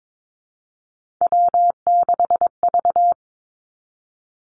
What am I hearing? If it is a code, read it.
Morse code, heard as a keyed tone, about 22 wpm, 700 Hz. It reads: W6V
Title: Three Challenging Letter-Number Combinations 22wpm